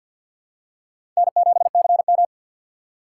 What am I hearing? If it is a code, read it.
Morse code, heard as a keyed tone, about 51 wpm, 700 Hz. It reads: N7CM
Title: USA Call Signs - 4 Characters - 6x Character Spacing - 50wpm